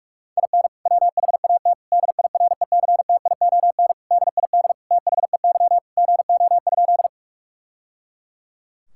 Morse code, heard as a keyed tone, about 46 wpm, 700 Hz. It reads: IN WHAT DIRECTION DID THEY GO?
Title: Sentences from Top 1000 Words - Review Part 5 - 45wpm